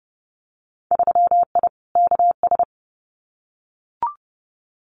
Morse code, heard as a keyed tone, about 30 wpm, 700 Hz. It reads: 3I KS
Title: Winter Field Day Exchanges Encore 30wpm